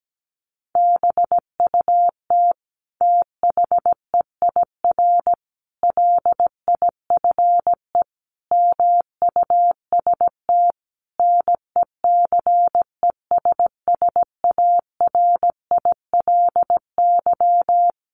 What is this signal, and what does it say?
Morse code, heard as a keyed tone, about 17 wpm, 700 Hz. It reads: BUT THEIR LIFE MUST NECESSARILY
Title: Sentences from Top 1400 Words 17wpm